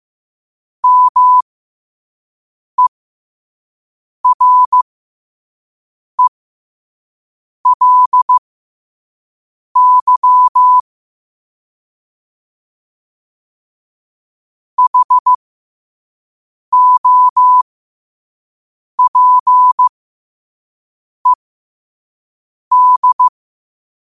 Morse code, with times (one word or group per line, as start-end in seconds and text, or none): 0.8-10.8 MERELY
14.8-23.3 HOPED